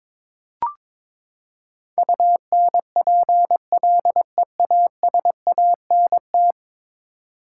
Morse code, heard as a keyed tone, about 22 wpm, 700 Hz. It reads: UNPLEASANT
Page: morse-code-ninja-practice-22wpm